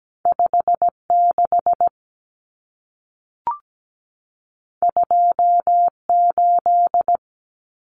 Morse code, heard as a keyed tone, about 17 wpm, 700 Hz. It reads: 56 28
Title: Numbers - 2-Digit 17wpm